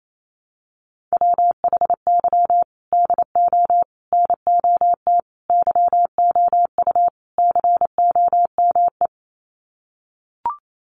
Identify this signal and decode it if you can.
Morse code, heard as a keyed tone, about 28 wpm, 700 Hz. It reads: WHY DO NOT YOU COME
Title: Sets of 5 Words Encore2 28wpm